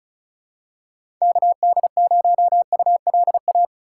Morse code, heard as a keyed tone, about 36 wpm, 700 Hz. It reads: KD0ULA
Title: USA Call Signs - 3x Character Spacing - 35wpm